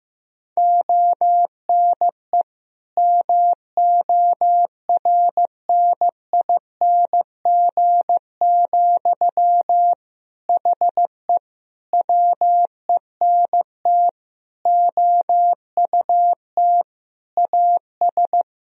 Morse code, heard as a keyed tone, about 15 wpm, 700 Hz. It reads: ONE MORNING, HE WENT OUT AS
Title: Sentences from Top 1000 Words - Review Part 2 - 15wpm